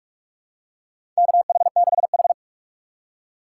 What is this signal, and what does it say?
Morse code, heard as a keyed tone, about 46 wpm, 700 Hz. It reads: KH6H
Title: USA Call Signs - 4 Characters - 4x Character Spacing - 45wpm